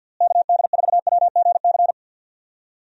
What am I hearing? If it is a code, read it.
Morse code, heard as a keyed tone, about 50 wpm, 700 Hz. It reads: KD4WGC
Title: USA Call Signs - 6 Characters - Repeated 3x - 50wpm